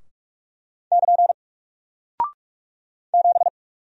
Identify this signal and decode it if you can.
Morse code, heard as a keyed tone, about 44 wpm, 700 Hz. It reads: <KN> 7
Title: Single Character (Letter:Number:Punctuation:Prosign) 45wpm